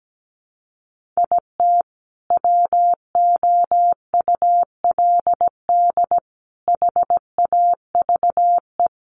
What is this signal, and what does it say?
Morse code, heard as a keyed tone, about 17 wpm, 700 Hz. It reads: IT WOULD HAVE
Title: Sentences from Top 1100 Words 17wpm